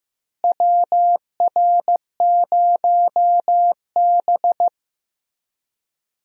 Morse code, heard as a keyed tone, about 15 wpm, 700 Hz. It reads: WR0B
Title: USA Call Signs - 4 Characters - Repeated 2x - 15wpm